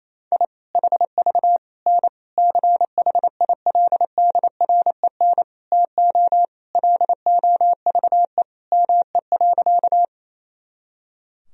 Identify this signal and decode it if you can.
Morse code, heard as a keyed tone, about 28 wpm, 700 Hz. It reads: I HV N CHILDREN TO LOVE ME.
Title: Sentences from Top 400 Words and Abbreviations 28wpm